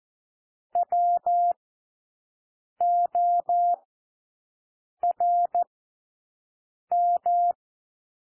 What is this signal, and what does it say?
Morse code, heard as a keyed tone, about 14 wpm, spaced as for 6 wpm, 700 Hz. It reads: WORM
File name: cwmp3 / W1 SOUND WORDS 14 wpm 0